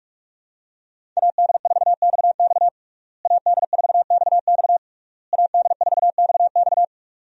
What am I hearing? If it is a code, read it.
Morse code, heard as a keyed tone, about 42 wpm, 700 Hz. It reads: AD4XX AD4XX AD4XX
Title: USA Call Signs - 5 Characters - Repeated 3x - 45wpm